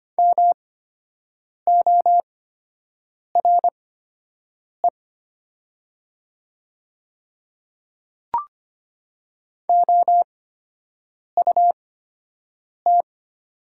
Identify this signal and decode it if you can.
Morse code, heard as a keyed tone, about 25 wpm, spaced as for 7 wpm, 700 Hz. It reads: MORE OUT
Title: 8x Character Spacing - Top 100 Words - 25wpm